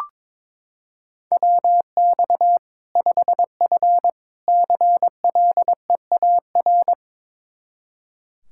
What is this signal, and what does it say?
Morse code, heard as a keyed tone, about 22 wpm, 700 Hz. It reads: WX 5F CLEAR
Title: Practice QSO - Exchange 2 - Info - 22wpm